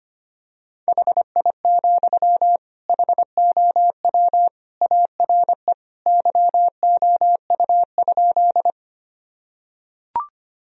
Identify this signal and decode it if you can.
Morse code, heard as a keyed tone, about 25 wpm, 700 Hz. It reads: HI, HOW ARE YOU?
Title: Sentences from Top 1000 Words - Review Part 3 - 25wpm